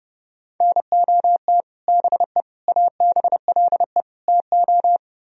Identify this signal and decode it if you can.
Morse code, heard as a keyed tone, about 40 wpm, spaced as for 35 wpm, 700 Hz. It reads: NOT BE ABLE TO
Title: Speed-Racing - Sets of 4 Words 30wpm